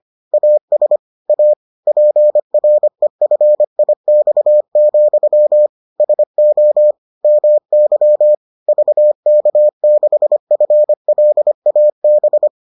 Morse code, heard as a keyed tone, about 25 wpm, 600 Hz. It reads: AS A PREFIX, SO MY VK6FLAB